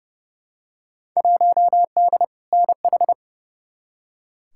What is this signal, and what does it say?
Morse code, heard as a keyed tone, about 31 wpm, 700 Hz. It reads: 1D NH
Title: Field Day Exchanges - 30wpm